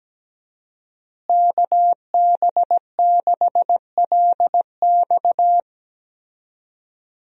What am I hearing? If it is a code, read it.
Morse code, heard as a keyed tone, about 17 wpm, 700 Hz. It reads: KB6LX
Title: USA Call Signs - 5 Characters - Repeated 2x - 17wpm